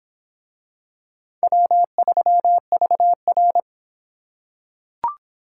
Morse code, heard as a keyed tone, about 25 wpm, 700 Hz. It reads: W3VR